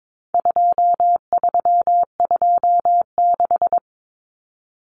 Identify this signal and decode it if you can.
Morse code, heard as a keyed tone, about 22 wpm, 700 Hz. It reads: 2326
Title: Common Cut Numbers - 4-Digit 22wpm